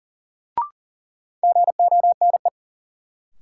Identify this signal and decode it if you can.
Morse code, heard as a keyed tone, about 40 wpm, 700 Hz. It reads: GONE
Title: Sentences from Top 1200 Words - New Word Drill - 40wpm